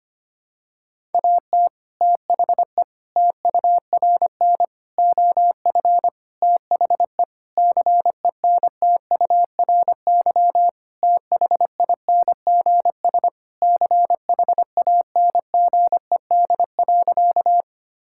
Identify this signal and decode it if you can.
Morse code, heard as a keyed tone, about 25 wpm, 700 Hz. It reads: AT THE TURN OF THE CENTURY THINGS CHANGED.
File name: practice / 25wpm / Sentences with 4-Word Phrases - 25wpm